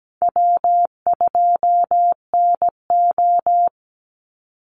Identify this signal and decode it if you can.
Morse code, heard as a keyed tone, about 17 wpm, 700 Hz. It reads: W2NO
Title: Easy Call Signs - 17wpm